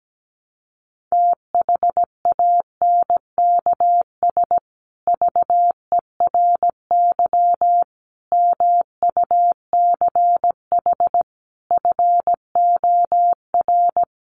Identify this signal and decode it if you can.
Morse code, heard as a keyed tone, about 17 wpm, 700 Hz. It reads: THANKS VERY MUCH FOR
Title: Sets of 5 Words Encore2 17wpm